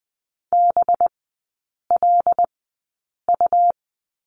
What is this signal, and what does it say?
Morse code, heard as a keyed tone, about 20 wpm, spaced as for 8 wpm, 700 Hz. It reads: BLU
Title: 184 Lesson 30e - Letter b - Three Characters 20wpm